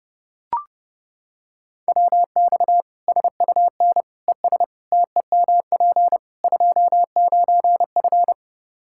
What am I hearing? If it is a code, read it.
Morse code, heard as a keyed tone, about 30 wpm, 700 Hz. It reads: WX SUN ES TEMP 29F
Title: K1USN SST QSO Practice 30wpm